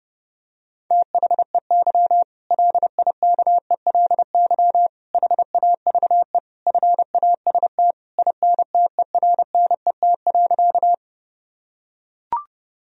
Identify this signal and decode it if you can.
Morse code, heard as a keyed tone, about 30 wpm, 700 Hz. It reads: THEY LIKELY HAVE FAST INTERNET.
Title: Sentences from Top 1000 Words - Review Part 4 - 30wpm